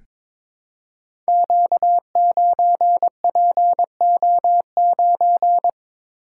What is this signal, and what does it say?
Morse code, heard as a keyed tone, about 22 wpm, 700 Hz. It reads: Q9PO9
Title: Code Groups - 5 Letter:Numbers - 22wpm